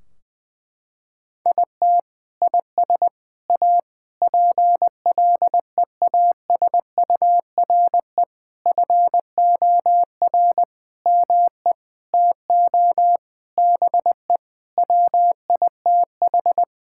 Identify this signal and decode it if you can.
Morse code, heard as a keyed tone, about 20 wpm, 700 Hz. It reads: IT IS A PLEASURE FOR ME TO BE WITH
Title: Sentences from Top 1600 Words 20wpm